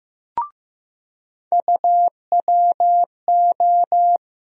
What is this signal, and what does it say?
Morse code, heard as a keyed tone, about 15 wpm, 700 Hz. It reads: UWO
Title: USA Call Sign Postfixes - 3 Letters 15wpm